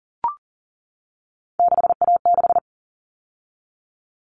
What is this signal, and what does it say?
Morse code, heard as a keyed tone, about 40 wpm, 700 Hz. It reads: 6A6
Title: All Cut Numbers - 3-Digit 40wpm